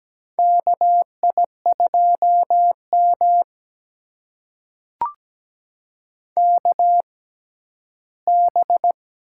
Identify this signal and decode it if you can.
Morse code, heard as a keyed tone, about 17 wpm, spaced as for 11 wpm, 700 Hz. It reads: KI2M KB
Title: USA Call Signs - 6x Character Spacing - 17wpm